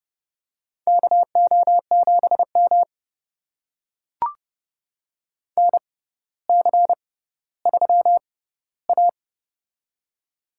Morse code, heard as a keyed tone, about 28 wpm, spaced as for 15 wpm, 700 Hz. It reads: KO7M NC3A
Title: USA Call Signs - 4 Characters - 6x Character Spacing - 30wpm